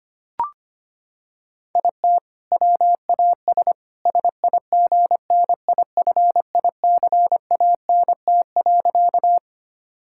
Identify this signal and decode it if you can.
Morse code, heard as a keyed tone, about 25 wpm, 700 Hz. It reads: IT WAS SIGNIFICANT.
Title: Sentences from Top 1000 Words - Review Part 6 - 25wpm